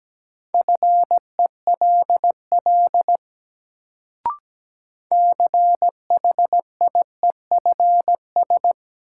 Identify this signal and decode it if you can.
Morse code, heard as a keyed tone, about 17 wpm, 700 Hz. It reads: FELL CHIEFS
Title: Sets of 5 Words Encore - New Word Drill - 17wpm